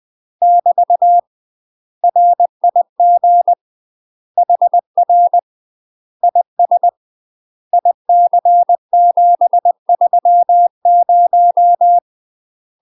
Morse code, heard as a keyed tone, about 20 wpm, 700 Hz. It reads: <BT> RIG HR IS IC730
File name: UK MASTER QSOs TRAINING / 20WPM / UK MASTER QSOs 0060